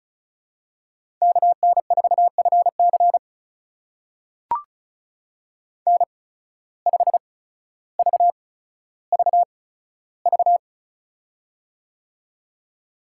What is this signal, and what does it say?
Morse code, heard as a keyed tone, about 36 wpm, spaced as for 15 wpm, 700 Hz. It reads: KN4FC N5VVV
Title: USA Call Signs - 5 Characters - 8x Character Spacing - 35wpm